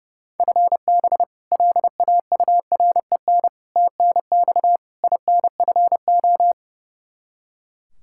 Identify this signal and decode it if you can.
Morse code, heard as a keyed tone, about 30 wpm, 700 Hz. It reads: FB LAUREN TNX INFO
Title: K1USN SST QSO Practice 30wpm